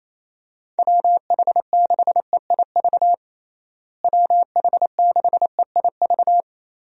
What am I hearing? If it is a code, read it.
Morse code, heard as a keyed tone, about 28 wpm, 700 Hz. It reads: WH6EIV WH6EIV
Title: USA Call Signs - 6 Characters - Repeated 3x - 28wpm